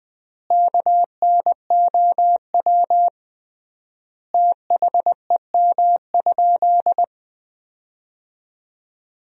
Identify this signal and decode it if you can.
Morse code, heard as a keyed tone, about 21 wpm, 700 Hz. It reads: KNOW THEM?
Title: 3x Word-Spacing - Sentences from Top 100 Words - 20wpm